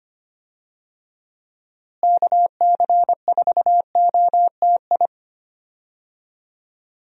Morse code, heard as a keyed tone, about 25 wpm, 700 Hz. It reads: KC4OTI
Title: USA Call Signs - 6 Characters - Repeated 2x - 25wpm